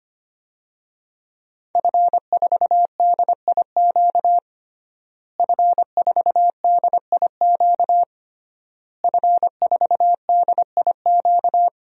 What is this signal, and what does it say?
Morse code, heard as a keyed tone, about 25 wpm, 700 Hz. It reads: F4DIQ F4DIQ F4DIQ
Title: DXCC-100 Call Signs - 5 Characters - Repeated 3x - 25wpm